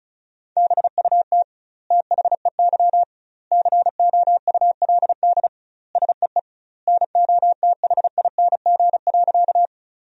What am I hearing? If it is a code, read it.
Morse code, heard as a keyed tone, about 36 wpm, 700 Hz. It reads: BUT THEY COULD SEE NOTHING.
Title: 2x Word-Spacing - Sentences from Top 500 Words - 35wpm